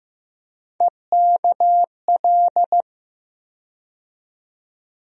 Morse code, heard as a keyed tone, about 15 wpm, 700 Hz. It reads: EKL
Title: USA Call Sign Postfixes - 3 Letters 15wpm